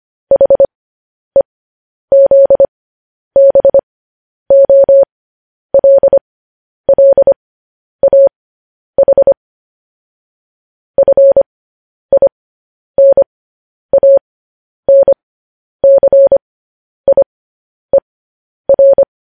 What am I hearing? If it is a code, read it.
Morse code, heard as a keyed tone, about 25 wpm, spaced as for 10 wpm, 550 Hz. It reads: HEZBOLLAH FINANCIER
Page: news-headlines-in-morse-code-at-10-wpm